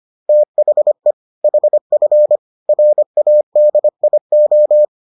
Morse code, heard as a keyed tone, about 25 wpm, 600 Hz. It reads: THE HF RADIO